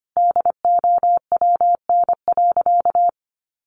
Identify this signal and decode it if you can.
Morse code, heard as a keyed tone, about 25 wpm, 700 Hz. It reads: DOWN.
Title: A Sound Of Thunder 25wpm